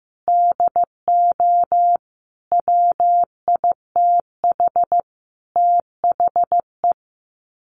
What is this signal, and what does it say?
Morse code, heard as a keyed tone, about 15 wpm, 700 Hz. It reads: DO WITH THE
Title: Sets of 5 Words Encore2 15wpm